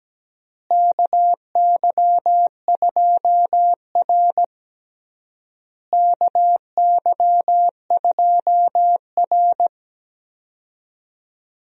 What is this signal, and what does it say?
Morse code, heard as a keyed tone, about 17 wpm, 700 Hz. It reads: KY2R KY2R
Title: USA Call Signs - 4 Characters - Repeated 3x - 17wpm